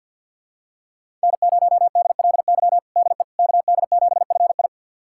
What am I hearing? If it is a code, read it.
Morse code, heard as a keyed tone, about 51 wpm, 700 Hz. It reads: N0DLY DE KD7FI
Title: USA Callsign DE Callsign - 5 Characters - 50wpm